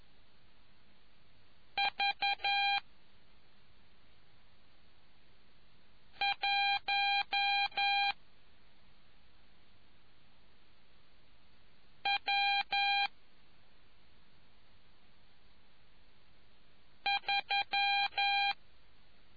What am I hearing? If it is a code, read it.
Morse code, heard as a keyed tone, about 11 wpm, 800 Hz. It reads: V 1 W 3